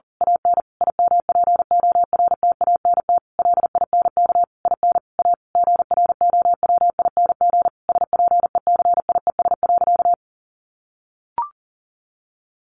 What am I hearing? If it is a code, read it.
Morse code, heard as a keyed tone, about 40 wpm, 700 Hz. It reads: AN IMPORTANT LINK IN A GROWING SPECIES.
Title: A Sound Of Thunder 40wpm